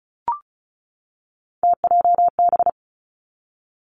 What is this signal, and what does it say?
Morse code, heard as a keyed tone, about 36 wpm, 700 Hz. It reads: TJB
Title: Three Challenging Letter-Number Combinations 35wpm